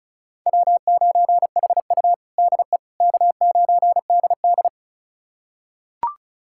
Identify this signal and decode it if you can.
Morse code, heard as a keyed tone, about 36 wpm, 700 Hz. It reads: W9HU DE K9DD
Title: USA Callsign DE Callsign - 4 Characters - 35wpm